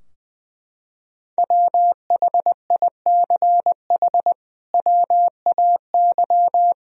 Morse code, heard as a keyed tone, about 20 wpm, 700 Hz. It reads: WHICH WAY
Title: Sentences from Top 1000 Words - Review Part 2 - 20wpm